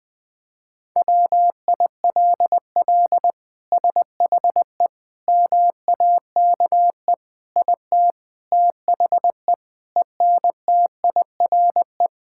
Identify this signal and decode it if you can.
Morse code, heard as a keyed tone, about 21 wpm, 700 Hz. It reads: WILL SHE MAKE IT THE ENTIRE
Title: Sentences from Top 1000 Words - Review Part 5 - 20wpm